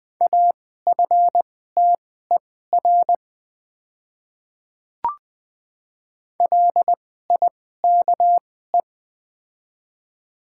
Morse code, heard as a keyed tone, about 20 wpm, spaced as for 15 wpm, 700 Hz. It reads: AFTER LIKE
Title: 2x Character Spacing - Top 300 Words - 20wpm